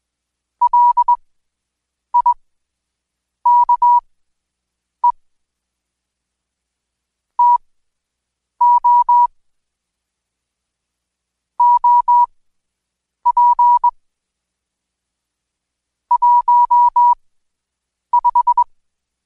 Morse code, heard as a keyed tone, about 20 wpm, spaced as for 7 wpm, 970 Hz. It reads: LIKE TO OP 15
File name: Code Quick / AudioDownld2 / Seg112-Day17-DayDrill 17